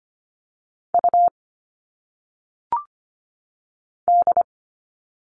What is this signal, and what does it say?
Morse code, heard as a keyed tone, about 25 wpm, 700 Hz. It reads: U D
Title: All Cut Numbers - 1-Digit 25wpm